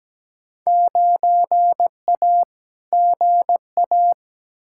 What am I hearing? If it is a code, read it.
Morse code, heard as a keyed tone, about 17 wpm, 700 Hz. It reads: 9A GA
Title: Field Day Exchanges Encore - 17wpm